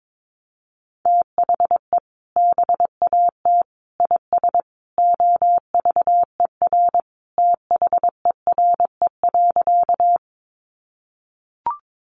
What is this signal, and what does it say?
Morse code, heard as a keyed tone, about 22 wpm, 700 Hz. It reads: THE BAT IS OVER THERE.
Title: Sentences from Top 1500 Words 22wpm